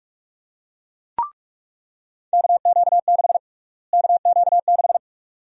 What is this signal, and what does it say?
Morse code, heard as a keyed tone, about 42 wpm, 700 Hz. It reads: KQ6 KQ6
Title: USA Call Sign Prefixes - Repeated 2x - 45wpm